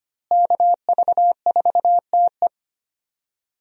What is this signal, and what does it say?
Morse code, heard as a keyed tone, about 25 wpm, 700 Hz. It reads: KV4TE
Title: USA Call Signs - 5 Characters - Repeated 2x - 25wpm